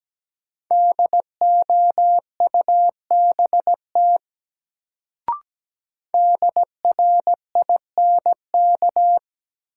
Morse code, heard as a keyed tone, about 17 wpm, 700 Hz. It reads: DOUBT DRINK
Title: Sentences from Top 1500 Words - New Word Drill - 17wpm